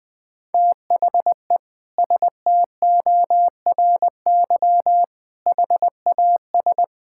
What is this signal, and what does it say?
Morse code, heard as a keyed tone, about 21 wpm, 700 Hz. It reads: THE STORY HAS
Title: Sentences with 8-Word Phrases - 20wpm